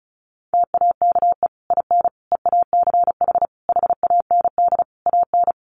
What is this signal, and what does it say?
Morse code, heard as a keyed tone, about 34 wpm, 700 Hz. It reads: TAKE IN EACH HAND AN
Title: Sentences from Top 1400 Words 35wpm